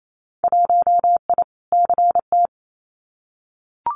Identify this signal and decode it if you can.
Morse code, heard as a keyed tone, about 28 wpm, 700 Hz. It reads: 1I CT
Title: Winter Field Day Exchanges 28wpm